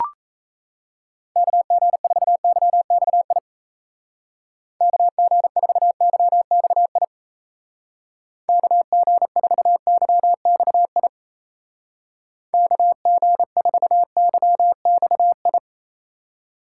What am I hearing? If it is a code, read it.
Morse code, heard as a keyed tone, about 37 wpm, 700 Hz. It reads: KG4YXI KG4YXI KG4YXI KG4YXI